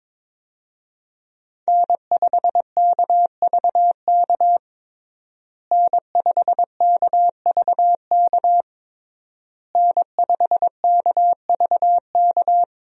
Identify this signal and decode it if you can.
Morse code, heard as a keyed tone, about 22 wpm, 700 Hz. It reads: N5KVK N5KVK N5KVK
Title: USA Call Signs - 5 Characters - Repeated 3x - 22wpm